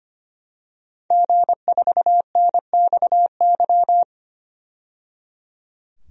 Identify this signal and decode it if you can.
Morse code, heard as a keyed tone, about 25 wpm, 700 Hz. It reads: G4NXY
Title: DXCC-100 Call Signs - 5 Characters - 25wpm